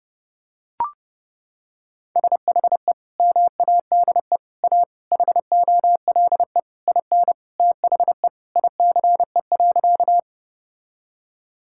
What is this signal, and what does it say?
Morse code, heard as a keyed tone, about 32 wpm, 700 Hz. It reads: SHE MADE A HOLE IN THE ICE.
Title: Sentences from Top 1600 Words 30wpm